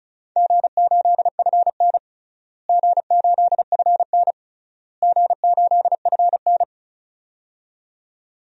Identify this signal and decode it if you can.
Morse code, heard as a keyed tone, about 36 wpm, 700 Hz. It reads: G8FN G8FN G8FN
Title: DXCC-100 Call Signs - 4 Characters - Repeated 3x - 35wpm